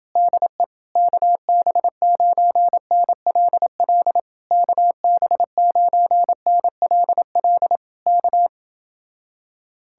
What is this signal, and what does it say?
Morse code, heard as a keyed tone, about 29 wpm, 700 Hz. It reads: DE KB9NLL KB9NLL K